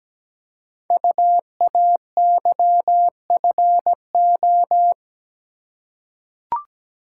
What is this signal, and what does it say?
Morse code, heard as a keyed tone, about 17 wpm, 700 Hz. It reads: UAYFO
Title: Code Groups - 5 Letters - 17wpm